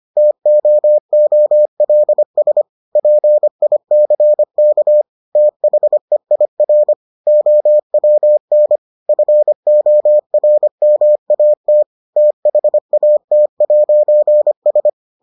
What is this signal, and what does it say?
Morse code, heard as a keyed tone, about 25 wpm, 600 Hz. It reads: TOOLS PICK THEIR OWN FORMAT THAT'S